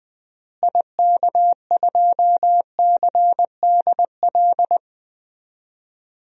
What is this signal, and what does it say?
Morse code, heard as a keyed tone, about 20 wpm, 700 Hz. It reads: IK2CDL
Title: DXCC-100 Call Signs - 6 Characters - Repeated 3x - 20wpm